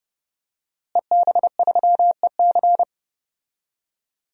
Morse code, heard as a keyed tone, about 29 wpm, 700 Hz. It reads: EB3EC
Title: DXCC-100 Call Signs - 5 Characters - Repeated 2x - 30wpm